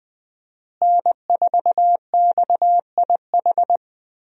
Morse code, heard as a keyed tone, about 21 wpm, 700 Hz. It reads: N4XIH
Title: USA Call Signs - 5 Characters - Repeated 2x - 20wpm